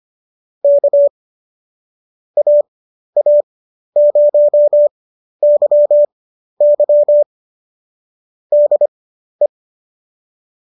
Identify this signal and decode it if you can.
Morse code, heard as a keyed tone, about 25 wpm, spaced as for 12 wpm, 590 Hz. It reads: K AA0YY DE